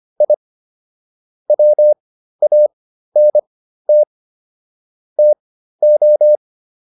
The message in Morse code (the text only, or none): I WANT TO